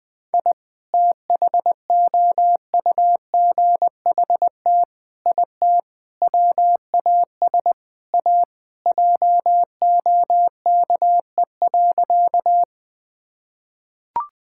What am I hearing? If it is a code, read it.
Morse code, heard as a keyed tone, about 20 wpm, 700 Hz. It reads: I THOUGHT IT WAS A JOKE.
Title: Sentences with 5-Word Phrases - 20wpm